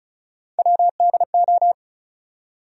Morse code, heard as a keyed tone, about 36 wpm, 700 Hz. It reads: WDO
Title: USA Call Sign Postfixes - 3 Letters 35wpm